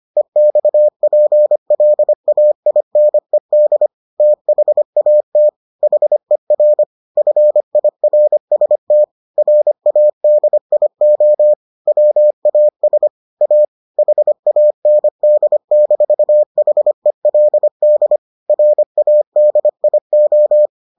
Morse code, heard as a keyed tone, about 25 wpm, 600 Hz. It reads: EXPLAINED THAT HER FIRST RADIO WAS A HAND-HELD RADIO